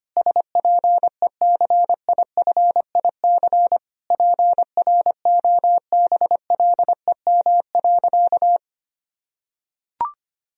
Morse code, heard as a keyed tone, about 25 wpm, 700 Hz. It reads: SPECIFIC PROBLEM.
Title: Sentences from Top 1000 Words - Review Part 3 - 25wpm